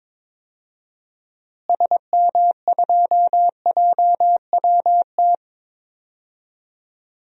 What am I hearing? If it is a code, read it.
Morse code, heard as a keyed tone, about 22 wpm, 700 Hz. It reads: SM2JWT
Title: DXCC-100 Call Signs - 6 Characters - Repeated 2x - 22wpm